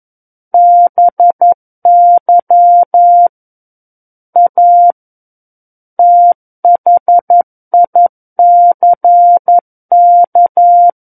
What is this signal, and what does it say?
Morse code, heard as a keyed tone, about 11 wpm, 700 Hz. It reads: BY A THICK
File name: 11 WPM